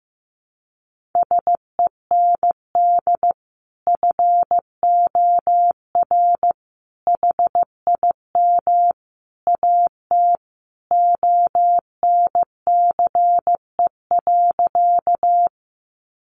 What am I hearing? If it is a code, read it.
Morse code, heard as a keyed tone, about 15 wpm, 700 Hz. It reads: SEND FOR HIM AT ONCE.
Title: Sentences from Top 1500 Words 15wpm